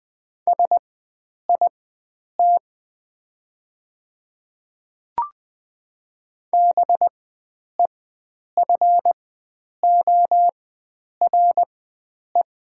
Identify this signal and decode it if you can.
Morse code, heard as a keyed tone, about 20 wpm, spaced as for 9 wpm, 700 Hz. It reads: SIT BEFORE
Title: 4x Character Spacing - Top 500 Words - 20wpm